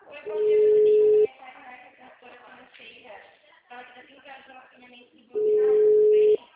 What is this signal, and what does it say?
Ringback tone (Europe, 425 Hz), with speech.
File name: out- -2006- - - .19691